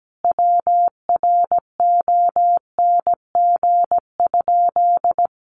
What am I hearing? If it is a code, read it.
Morse code, heard as a keyed tone, about 17 wpm, 700 Hz. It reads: WRONG?
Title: Sentences from Top 900 Words 17wpm